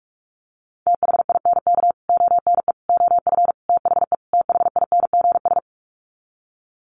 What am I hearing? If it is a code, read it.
Morse code, heard as a keyed tone, about 44 wpm, 700 Hz. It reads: THINK ONE OF THE THINGS
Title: Sets of 5 Words Encore2 45wpm